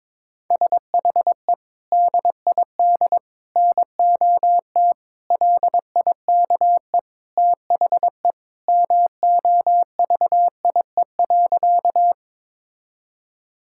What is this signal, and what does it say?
Morse code, heard as a keyed tone, about 22 wpm, 700 Hz. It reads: SHE DID NOT LIKE THE MOVIE.
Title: Sentences from Top 1000 Words - Review Part 5 - 22wpm